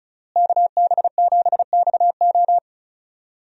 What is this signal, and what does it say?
Morse code, heard as a keyed tone, about 36 wpm, 700 Hz. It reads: KB7XO
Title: USA Call Signs - 5 Characters - Repeated 3x - 35wpm